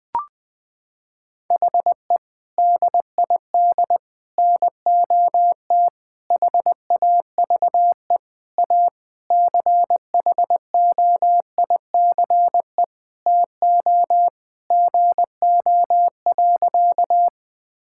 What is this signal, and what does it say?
Morse code, heard as a keyed tone, about 20 wpm, 700 Hz. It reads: HE DID NOT HAVE A CHOICE TO GO.
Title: Sentences with 4-Word Phrases - 20wpm